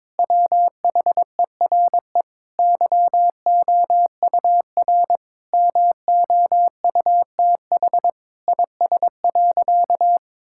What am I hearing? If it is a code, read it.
Morse code, heard as a keyed tone, about 21 wpm, 700 Hz. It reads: WHERE YOUR MOUTH IS.
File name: Sentences with 7-Word Phrases - 22wpm